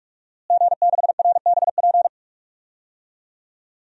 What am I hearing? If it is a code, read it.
Morse code, heard as a keyed tone, about 46 wpm, 700 Hz. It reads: G6RBP
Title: DXCC-100 Call Signs - 5 Characters - Repeated 2x - 45wpm